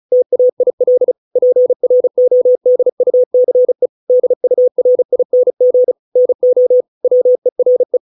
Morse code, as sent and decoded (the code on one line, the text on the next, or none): - .- .. .-.. / .--. .-. --- -.. ..- -.-. / -.. ..- .-. .. -. --. / -. --- / .-- . .-. .
TAIL PRODUCE DURING NO WERE